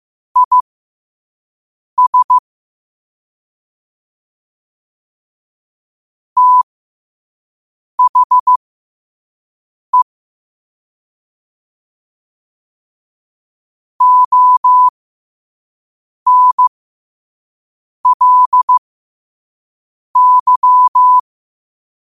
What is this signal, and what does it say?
Morse code, heard as a keyed tone, about 15 wpm, spaced as for 5 wpm, 1000 Hz. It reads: IS THE ONLY
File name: qotdfiles / 5wpm